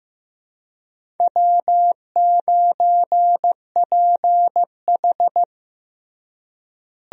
Morse code, heard as a keyed tone, about 15 wpm, 700 Hz. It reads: W9PH
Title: USA Call Signs - 4 Characters - Repeated 3x - 15wpm